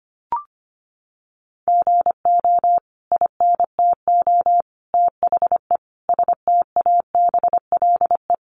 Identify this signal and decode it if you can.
Morse code, heard as a keyed tone, about 25 wpm, 700 Hz. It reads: GO INTO THE STABLE
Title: Sentences from Top 1500 Words 25wpm